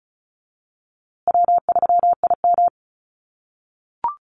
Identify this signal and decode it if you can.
Morse code, heard as a keyed tone, about 34 wpm, 700 Hz. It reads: W3IM
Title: USA Call Signs Repeated 2x 35wpm